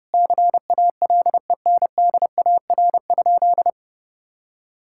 Morse code, heard as a keyed tone, about 30 wpm, 700 Hz. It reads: CALENDAR?
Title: Sentences from Top 1600 Words 30wpm